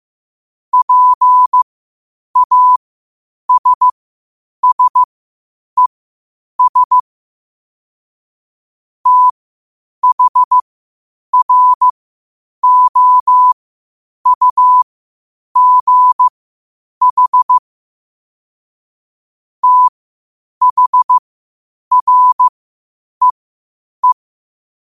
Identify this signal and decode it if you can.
Morse code, heard as a keyed tone, about 15 wpm, spaced as for 8 wpm, 1000 Hz. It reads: PASSES THROUGH THREE
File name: qotdfiles / 8wpm